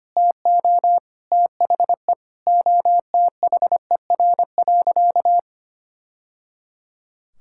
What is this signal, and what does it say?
Morse code, heard as a keyed tone, about 25 wpm, 700 Hz. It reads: TO THE OTHER.
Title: Sentences from Top 1000 Words - Review Part 5 - 25wpm